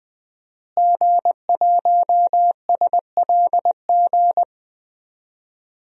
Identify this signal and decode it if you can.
Morse code, heard as a keyed tone, about 20 wpm, 700 Hz. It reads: G1SLG
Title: DXCC-100 Call Signs - 5 Characters - Repeated 2x - 20wpm